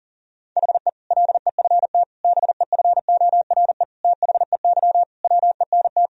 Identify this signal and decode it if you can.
Morse code, heard as a keyed tone, about 40 wpm, 700 Hz. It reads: HE LEFT BEFORE THEY WENT
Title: Sentences from Top 1000 Words - Review Part 4 - 40wpm